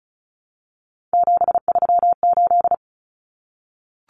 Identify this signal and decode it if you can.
Morse code, heard as a keyed tone, about 34 wpm, 700 Hz. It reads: 738
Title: Numbers - 3-Digit 35wpm